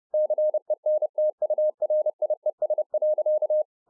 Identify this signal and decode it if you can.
Morse code, heard as a keyed tone, about 30 wpm, 610 Hz. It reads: CENTURIES.